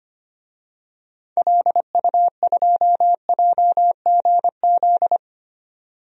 Morse code, heard as a keyed tone, about 25 wpm, 700 Hz. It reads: LU2JGZ
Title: DXCC-100 Call Signs - 6 Characters - 25wpm